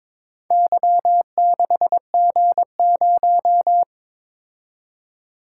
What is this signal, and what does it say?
Morse code, heard as a keyed tone, about 22 wpm, 700 Hz. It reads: Y6G0
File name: practice / 22wpm / Four Letter-Number Combinations - 22wpm